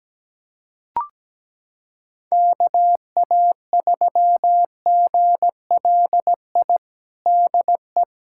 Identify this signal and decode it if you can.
Morse code, heard as a keyed tone, about 17 wpm, 700 Hz. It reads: KA3GLI DE
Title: USA Callsign DE Callsign - 6 Characters - 17wpm